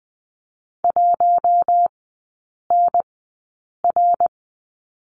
Morse code, heard as a keyed tone, about 20 wpm, spaced as for 8 wpm, 700 Hz. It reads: 1NR
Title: 053 Lesson 10e - Letter r - Three Characters 20wpm